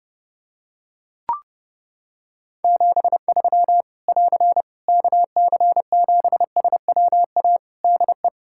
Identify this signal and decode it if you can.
Morse code, heard as a keyed tone, about 30 wpm, 700 Hz. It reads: 73 <AR> KC7SWA DE
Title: Practice QSO - Exchange 4 - Ending - 30wpm